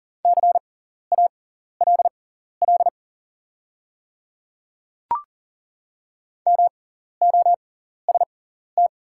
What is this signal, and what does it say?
Morse code, heard as a keyed tone, about 40 wpm, spaced as for 14 wpm, 700 Hz. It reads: CALL MOST
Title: 6x Character Spacing - Top 100 Words - 40wpm